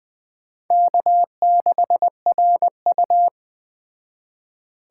Morse code, heard as a keyed tone, about 20 wpm, 700 Hz. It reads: K6RU
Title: Speed Racing - USA 1x2 Call Signs 17wpm